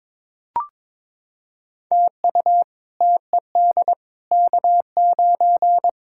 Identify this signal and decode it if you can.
Morse code, heard as a keyed tone, about 22 wpm, 700 Hz. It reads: TU TED K9
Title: K1USN SST QSO Practice - 22wpm